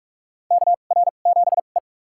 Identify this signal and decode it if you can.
Morse code, heard as a keyed tone, about 42 wpm, spaced as for 31 wpm, 700 Hz. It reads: KR7E
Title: USA Call Signs - 2x Character Spacing - 45wpm